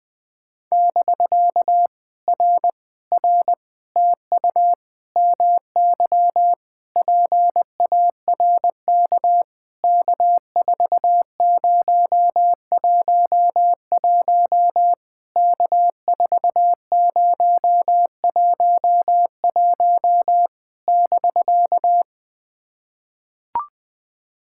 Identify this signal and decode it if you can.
Morse code, heard as a keyed tone, about 20 wpm, 700 Hz. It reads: <BK> R R TU MY PARK K4011 K4011 <BK>
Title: POTA QSO Practice - 20wpm